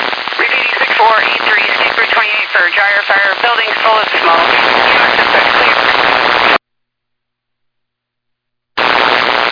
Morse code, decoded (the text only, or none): none